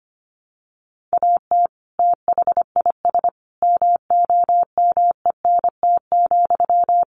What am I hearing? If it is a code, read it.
Morse code, heard as a keyed tone, about 25 wpm, 700 Hz. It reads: AT THIS MOMENT,
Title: Sentences from Top 1000 Words 25wpm